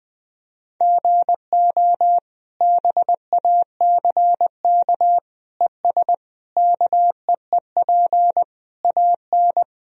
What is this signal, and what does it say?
Morse code, heard as a keyed tone, about 21 wpm, 700 Hz. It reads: GO BACK ES KEEP AN
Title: Sentences from Top 300 Words and Abbreviations 20wpm